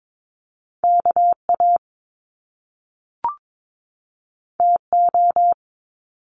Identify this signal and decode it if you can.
Morse code, heard as a keyed tone, about 22 wpm, 700 Hz. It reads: KA TO
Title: Top Two Letter Combinations in Words 22wpm